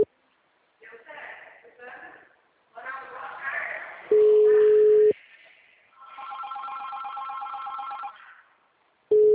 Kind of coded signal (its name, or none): telephone call progress tone